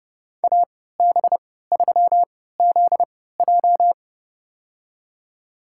Morse code, heard as a keyed tone, about 30 wpm, spaced as for 17 wpm, 700 Hz. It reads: AB3ZJ
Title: USA Call Signs - 5 Characters - 3x Character Spacing - 30wpm